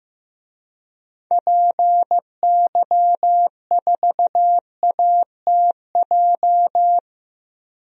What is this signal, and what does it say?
Morse code, heard as a keyed tone, about 15 wpm, 700 Hz. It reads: PY4ATJ
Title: DXCC-100 Call Signs - 6 Characters - 15wpm